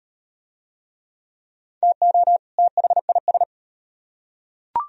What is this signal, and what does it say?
Morse code, heard as a keyed tone, about 38 wpm, 700 Hz. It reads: TO THIS